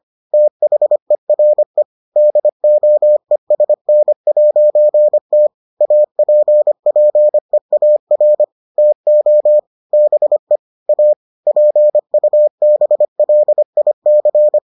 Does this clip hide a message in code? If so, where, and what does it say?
Yes: Morse code from 0.3 to 14.6 s, heard as a keyed tone, about 25 wpm, 600 Hz. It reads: THERE DOESN'T APPEAR TO BE A PUBLIC